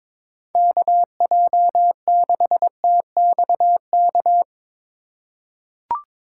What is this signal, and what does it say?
Morse code, heard as a keyed tone, about 21 wpm, 700 Hz. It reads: KJ6TXK
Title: USA Call Signs - 6 Characters - 8x Character Spacing - 22wpm